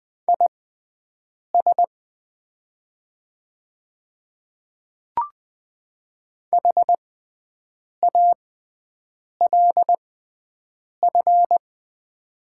Morse code, heard as a keyed tone, about 20 wpm, spaced as for 7 wpm, 700 Hz. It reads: IS HALF
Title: 6x Character Spacing - Top 500 Words - 20wpm